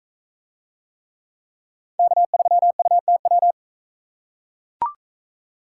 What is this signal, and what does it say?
Morse code, heard as a keyed tone, about 44 wpm, 700 Hz. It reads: K3UTW